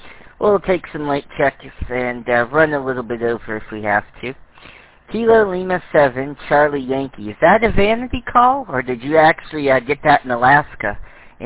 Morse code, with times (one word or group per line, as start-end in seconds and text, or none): none